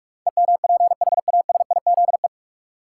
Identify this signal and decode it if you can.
Morse code, heard as a keyed tone, about 42 wpm, 700 Hz. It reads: EMPHASIZE
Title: Sentences from Top 1500 Words - New Word Drill Alt - 45wpm